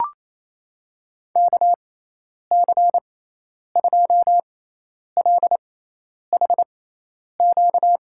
Morse code, heard as a keyed tone, about 28 wpm, spaced as for 10 wpm, 700 Hz. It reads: KC2LHQ
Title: USA Call Signs - 6 Characters - 6x Character Spacing - 28wpm